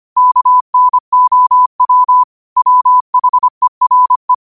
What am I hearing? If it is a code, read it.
Morse code, heard as a keyed tone, about 25 wpm, 1000 Hz. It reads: KNOW WHERE
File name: qotdfiles / 25wpm